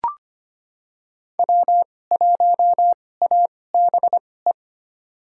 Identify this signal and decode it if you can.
Morse code, heard as a keyed tone, about 25 wpm, spaced as for 18 wpm, 700 Hz. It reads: W1ABE
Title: USA Call Signs - 2x Character Spacing - 25wpm